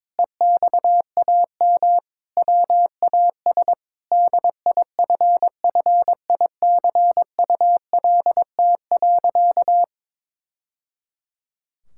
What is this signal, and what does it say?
Morse code, heard as a keyed tone, about 21 wpm, 700 Hz. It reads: EXAM WAS DIFFICULT.
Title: Sentences from Top 1000 Words - Review Part 5 - 22wpm